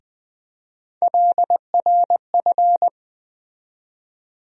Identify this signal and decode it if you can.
Morse code, heard as a keyed tone, about 20 wpm, 700 Hz. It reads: LRF
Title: USA Call Sign Postfixes - 3 Letters 20wpm